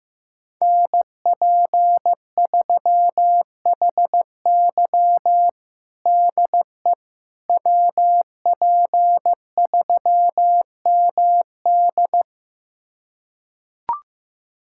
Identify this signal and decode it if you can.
Morse code, heard as a keyed tone, about 15 wpm, 700 Hz. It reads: NP3HY DE WP3MD
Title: USA Callsign DE Callsign - 5 Characters - 15wpm